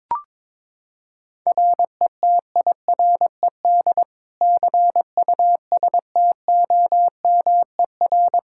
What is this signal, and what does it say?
Morse code, heard as a keyed tone, about 22 wpm, 700 Hz. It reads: RETIRED CUSTOMER
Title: Practice QSO - Exchange 3 - More Info - 22wpm